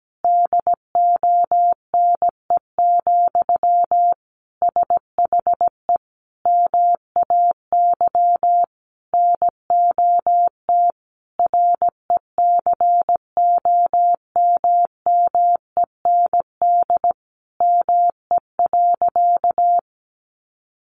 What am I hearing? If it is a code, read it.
Morse code, heard as a keyed tone, about 17 wpm, 700 Hz. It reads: DONE, SHE MAY NOT RECOMMEND ME.
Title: Sentences from Top 1200 Words 17wpm